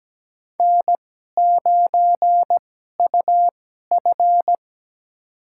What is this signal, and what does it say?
Morse code, heard as a keyed tone, about 17 wpm, spaced as for 12 wpm, 700 Hz. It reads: N9UF
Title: USA Call Signs - 4 Characters - 2x Character Spacing - 17wpm